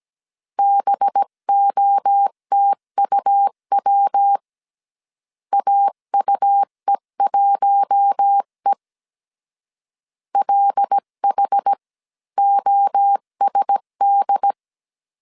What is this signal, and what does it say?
Morse code, heard as a keyed tone, about 17 wpm, 800 Hz. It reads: BOTUW AUE1E LHOSD
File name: CW / DET / 1 6 1500C S C 16wpm 2733